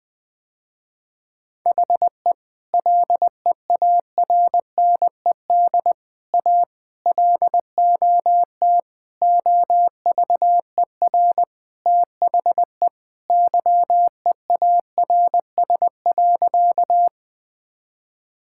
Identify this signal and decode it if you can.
Morse code, heard as a keyed tone, about 20 wpm, 700 Hz. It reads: HE LEARNED A LOT OVER THE YEARS.
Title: Sentences with 3-Word Phrases - Spoken Beginning - 20wpm